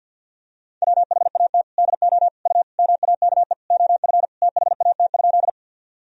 Morse code, heard as a keyed tone, about 51 wpm, 700 Hz. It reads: WHAT DO U MAKE OF THAT?
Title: Sentences from Top 100 Words and Abbreviations 50wpm